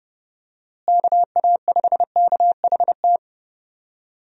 Morse code, heard as a keyed tone, about 30 wpm, 700 Hz. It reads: KA5KHT
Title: USA Call Signs - 6 Characters - Repeated 3x - 30wpm